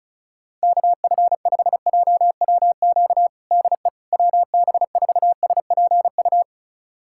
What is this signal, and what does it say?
Morse code, heard as a keyed tone, about 36 wpm, 700 Hz. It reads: KF5JWQ DE WB4SPU
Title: USA Callsign DE Callsign - 6 Characters - 35wpm